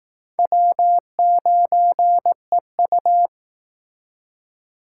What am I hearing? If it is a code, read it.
Morse code, heard as a keyed tone, about 18 wpm, 700 Hz. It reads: W9EU
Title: Speed Racing - USA 1x2 Call Signs 15wpm